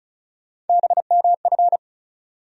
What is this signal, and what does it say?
Morse code, heard as a keyed tone, about 34 wpm, 700 Hz. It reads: BMF
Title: USA Call Sign Postfixes - 3 Letters 35wpm